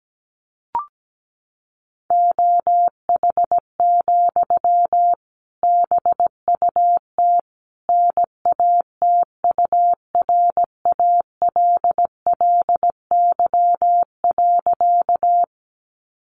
Morse code, heard as a keyed tone, about 17 wpm, 700 Hz. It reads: OH, BUT NATURALLY.
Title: Sentences from Top 1300 Words 17wpm